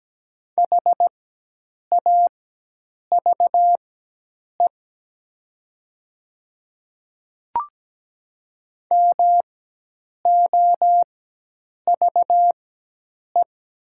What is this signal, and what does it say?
Morse code, heard as a keyed tone, about 17 wpm, spaced as for 8 wpm, 700 Hz. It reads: HAVE MOVE
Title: 4x Character Spacing - Top 200 Words - 17wpm